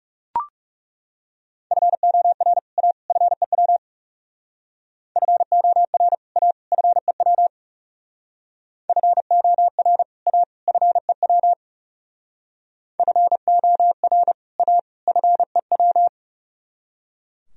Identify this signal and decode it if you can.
Morse code, heard as a keyed tone, about 36 wpm, 700 Hz. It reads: FOR A FEW FOR A FEW FOR A FEW FOR A FEW
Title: Speed-Racing - Sets of 3 Words 30wpm